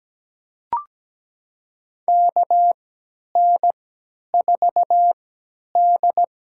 Morse code, heard as a keyed tone, about 17 wpm, spaced as for 10 wpm, 700 Hz. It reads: KN4D
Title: USA Call Signs - 3x Character Spacing - 17wpm